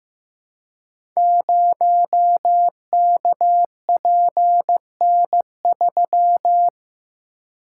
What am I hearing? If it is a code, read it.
Morse code, heard as a keyed tone, about 15 wpm, 700 Hz. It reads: 0KPN3
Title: Code Groups - 5 Letter:Numbers - 15wpm